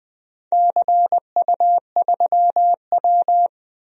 Morse code, heard as a keyed tone, about 19 wpm, 700 Hz. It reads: CU3W
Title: DXCC-100 Call Signs - 4 Characters - Repeated 3x - 20wpm